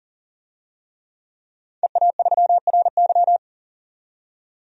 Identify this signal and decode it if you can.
Morse code, heard as a keyed tone, about 40 wpm, 700 Hz. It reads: EA3RY
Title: DXCC-100 Call Signs - 5 Characters - 40wpm